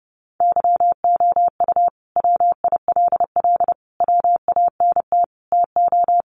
Morse code, heard as a keyed tone, about 30 wpm, 700 Hz. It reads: YOU WILL WANT TO
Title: Sentences from Top 700 Words 30wpm